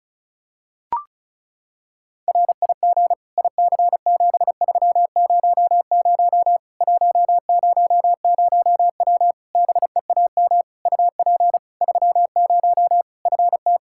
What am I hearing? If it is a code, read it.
Morse code, heard as a keyed tone, about 34 wpm, 700 Hz. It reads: RIG IC7300 100W BEAM UP 30 FT
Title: Practice QSO - Exchange 2 - Info - 35wpm